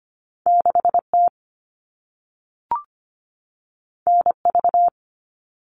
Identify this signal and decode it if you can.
Morse code, heard as a keyed tone, about 25 wpm, 700 Hz. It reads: 6T NV
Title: All Cut Numbers - 2-Digit 25wpm